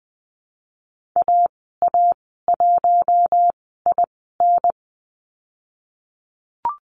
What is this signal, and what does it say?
Morse code, heard as a keyed tone, about 20 wpm, 700 Hz. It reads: A A 1 I N
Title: 055 Lesson 10g - Letter r - Call Signs 20wpm